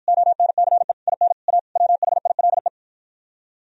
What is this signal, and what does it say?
Morse code, heard as a keyed tone, about 52 wpm, 700 Hz. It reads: ONCE IN A WHILE